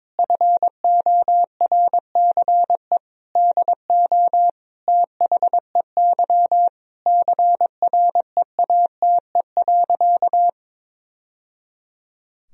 Morse code, heard as a keyed tone, about 22 wpm, 700 Hz. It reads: FORCE DO THEY CREATE.
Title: Sentences from Top 1000 Words - Review Part 3 - 22wpm